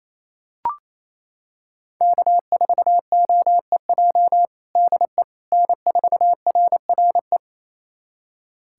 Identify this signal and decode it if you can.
Morse code, heard as a keyed tone, about 28 wpm, 700 Hz. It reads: K4OEJ DE N4RRE
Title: USA Callsign DE Callsign - 5 Characters - 28wpm